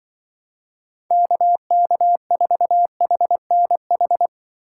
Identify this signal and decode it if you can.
Morse code, heard as a keyed tone, about 24 wpm, 700 Hz. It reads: KK4HNH
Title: Speed Racing - USA 2x3 Call Signs 20wpm